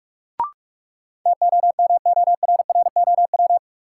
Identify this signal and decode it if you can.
Morse code, heard as a keyed tone, about 44 wpm, 700 Hz. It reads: TOMORROW
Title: Sentences from Top 1100 Words - New Word Drill Alt - 45wpm